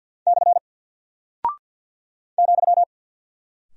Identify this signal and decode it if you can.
Morse code, heard as a keyed tone, about 48 wpm, 700 Hz. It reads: / ,
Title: Single Character (Letter:Number:Punctuation:Prosign) Alt 50wpm